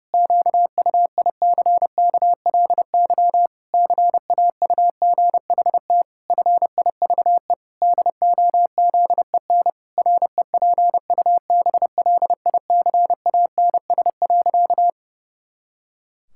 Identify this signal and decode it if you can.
Morse code, heard as a keyed tone, about 32 wpm, 700 Hz. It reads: QUICKLY CAUGHT FIVE DOZEN REPUBLICANS.
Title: Pangram Challenge v2 - 30wpm